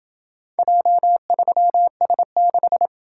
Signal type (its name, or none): Morse code